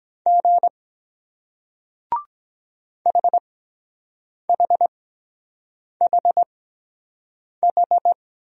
Morse code, heard as a keyed tone, about 23 wpm, spaced as for 7 wpm, 700 Hz. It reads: G HHHH